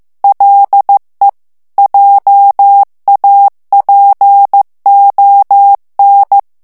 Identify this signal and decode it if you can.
Morse code, heard as a keyed tone, about 15 wpm, 800 Hz. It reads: LE JAPON
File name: morse mp3 / FRANCAIS TEXTE CLAIR 1 VITESSE 14 MOTS MINUTES